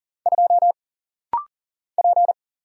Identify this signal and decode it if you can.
Morse code, heard as a keyed tone, about 39 wpm, 700 Hz. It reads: 2 P
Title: Single Letter-Number - Rapid-Fire 40wpm